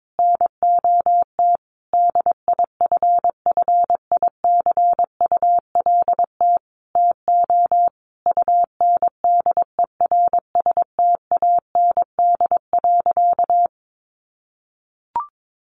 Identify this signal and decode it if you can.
Morse code, heard as a keyed tone, about 22 wpm, 700 Hz. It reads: NOT DIFFICULT TO UNDERSTAND.
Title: Sentences from Top 1100 Words 22wpm